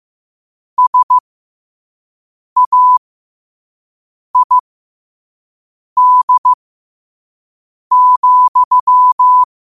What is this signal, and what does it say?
Morse code, heard as a keyed tone, about 15 wpm, spaced as for 5 wpm, 1000 Hz. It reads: SAID,
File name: qotdfiles / 5wpm